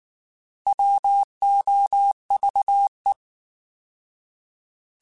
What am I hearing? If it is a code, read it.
Morse code, heard as a keyed tone, about 19 wpm, 800 Hz. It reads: WOVE
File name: Alphabetical / Alphabetical mp3s / Ws @18